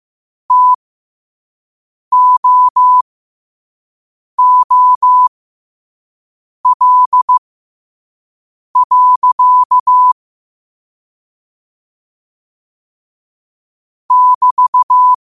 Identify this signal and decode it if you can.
Morse code, heard as a keyed tone, about 15 wpm, spaced as for 5 wpm, 1000 Hz. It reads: TOOL. <BT>
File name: qotdfiles / 5wpm